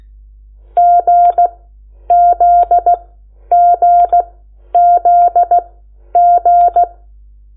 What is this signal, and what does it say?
Morse code, heard as a keyed tone, about 16 wpm, spaced as for 10 wpm, 700 Hz. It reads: GZGZG
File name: Lecciones / CLASE Nº8